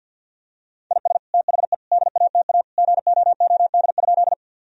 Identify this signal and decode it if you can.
Morse code, heard as a keyed tone, about 50 wpm, 700 Hz. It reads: IS THE DATA GOOD?
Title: Sentences from Top 1000 Words - Review Part 3 - 50wpm